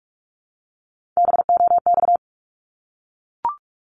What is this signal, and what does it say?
Morse code, heard as a keyed tone, about 44 wpm, 700 Hz. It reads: BOX
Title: Top 500 Words 45wpm